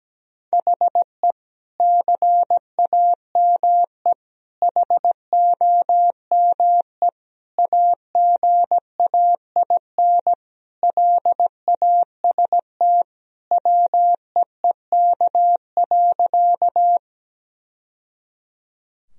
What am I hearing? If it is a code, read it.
Morse code, heard as a keyed tone, about 17 wpm, 700 Hz. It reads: HE CAME HOME AGAIN LAST WEEK.
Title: Sentences from Top 1000 Words - Review Part 2 - 17wpm